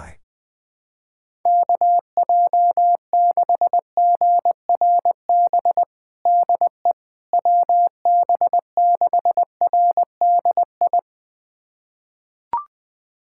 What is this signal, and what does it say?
Morse code, heard as a keyed tone, about 21 wpm, 700 Hz. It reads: KJ6GRB DE WB6RDI
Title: USA Callsign DE Callsign - 6 Characters - 20wpm